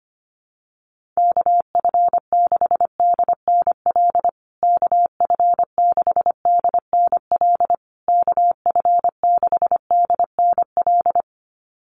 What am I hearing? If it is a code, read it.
Morse code, heard as a keyed tone, about 25 wpm, 700 Hz. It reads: KF6DNL KF6DNL KF6DNL
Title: USA Call Signs Repeated 3x 25wpm